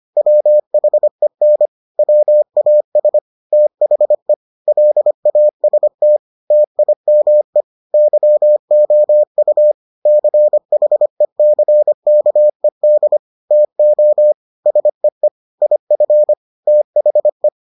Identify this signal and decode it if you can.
Morse code, heard as a keyed tone, about 25 wpm, 600 Hz. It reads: WHEN WAS THE LAST TIME YOU CHECKED TO SEE IF THE